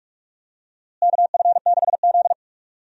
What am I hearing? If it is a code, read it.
Morse code, heard as a keyed tone, about 41 wpm, 700 Hz. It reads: KF6Z
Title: USA Call Signs - 4 Characters - Repeated 2x - 45wpm